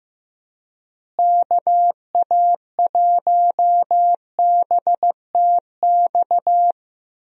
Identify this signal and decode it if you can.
Morse code, heard as a keyed tone, about 15 wpm, 700 Hz. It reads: KA1BTX
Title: USA Call Signs - 6 Characters - Repeated 2x - 15wpm